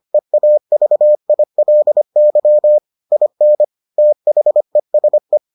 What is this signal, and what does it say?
Morse code, heard as a keyed tone, about 25 wpm, 600 Hz. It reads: EAVILY IN THESE